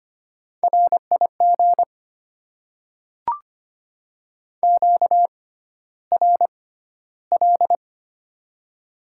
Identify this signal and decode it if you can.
Morse code, heard as a keyed tone, about 25 wpm, spaced as for 13 wpm, 700 Hz. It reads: RIG QRL
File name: Top 20 QSO Elements - 6x Character Spacing - 25wpm